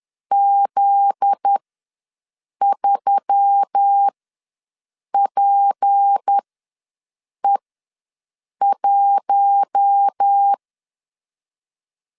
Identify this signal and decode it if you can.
Morse code, heard as a keyed tone, about 10 wpm, spaced as for 6 wpm, 800 Hz. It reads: Z3PE1
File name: LEZIONI / 8 YJ45 / 8 YJ4 ws 10wpm 4020